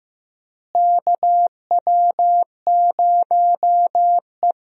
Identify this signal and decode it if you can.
Morse code, heard as a keyed tone, about 15 wpm, 700 Hz. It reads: KW0E
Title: USA Call Signs - 4 Characters - Repeated 3x - 15wpm